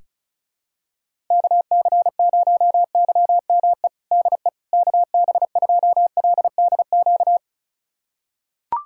Morse code, heard as a keyed tone, about 35 wpm, 700 Hz. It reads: KC0YME DE KB2LDQ
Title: USA Callsign DE Callsign - 6 Characters - 35wpm